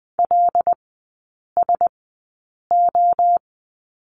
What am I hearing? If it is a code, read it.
Morse code, heard as a keyed tone, about 20 wpm, spaced as for 8 wpm, 700 Hz. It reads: LSO
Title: 191 Lesson 31e - Letter v - Three Characters 20wpm